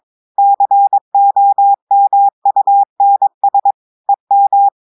Morse code, heard as a keyed tone, about 22 wpm, 800 Hz. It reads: COMUNS EM
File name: audios / portugais / portugais texte en clair 2 vitesse 22 mots minutes